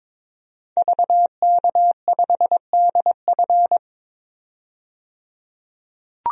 Morse code, heard as a keyed tone, about 22 wpm, 700 Hz. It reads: VK5DF
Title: DXCC-100 Call Signs - 5 Characters - 22wpm